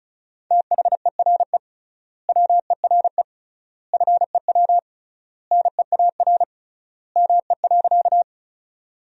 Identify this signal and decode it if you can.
Morse code, heard as a keyed tone, about 36 wpm, 700 Hz. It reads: THERE WERE FEW NEAR ME.
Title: 3x Word-Spacing - Sentences from Top 300 Words - 35wpm